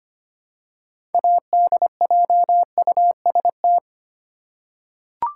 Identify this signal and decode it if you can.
Morse code, heard as a keyed tone, about 25 wpm, 700 Hz. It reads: ADJUST
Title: Six Letter Words - 25wpm